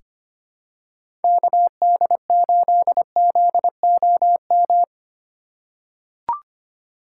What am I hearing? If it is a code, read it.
Morse code, heard as a keyed tone, about 25 wpm, 700 Hz. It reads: KD8ZOM
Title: USA Call Signs - 3x Character Spacing - 25wpm